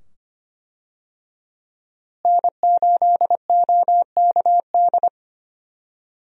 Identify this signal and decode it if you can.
Morse code, heard as a keyed tone, about 25 wpm, 700 Hz. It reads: N8OKD
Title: USA Call Signs - 5 Characters - Repeated 2x - 25wpm